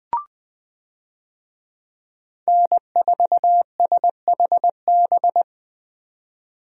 Morse code, heard as a keyed tone, about 20 wpm, 700 Hz. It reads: N4SHB
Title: USA Call Signs - 5 Characters - 20wpm